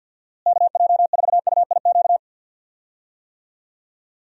Morse code, heard as a keyed tone, about 46 wpm, 700 Hz. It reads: KJ4UIQ
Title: USA Call Signs - 6 Characters - Repeated 2x - 50wpm